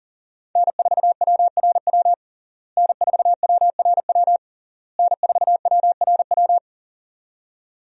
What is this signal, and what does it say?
Morse code, heard as a keyed tone, about 40 wpm, 700 Hz. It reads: N4WRW N4WRW N4WRW
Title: USA Call Signs - 5 Characters - Repeated 3x - 40wpm